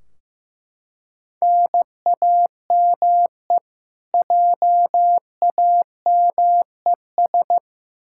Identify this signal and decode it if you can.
Morse code, heard as a keyed tone, about 15 wpm, 700 Hz. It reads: NAME JAMES
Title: Practice QSO - Exchange 1 - Beginning - 15wpm